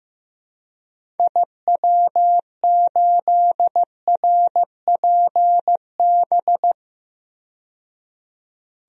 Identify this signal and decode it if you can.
Morse code, heard as a keyed tone, about 15 wpm, 700 Hz. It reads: IW8RPB
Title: DXCC-100 Call Signs - 6 Characters - 15wpm